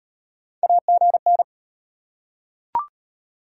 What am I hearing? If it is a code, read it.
Morse code, heard as a keyed tone, about 36 wpm, 700 Hz. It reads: AGN